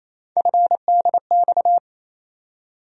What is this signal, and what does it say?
Morse code, heard as a keyed tone, about 28 wpm, 700 Hz. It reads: FDX
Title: USA Call Sign Postfixes - 3 Letters 28wpm